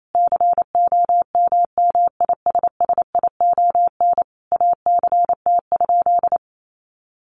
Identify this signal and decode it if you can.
Morse code, heard as a keyed tone, about 28 wpm, 700 Hz. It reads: COMMISSION ACT?
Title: Sentences from Top 1400 Words 28wpm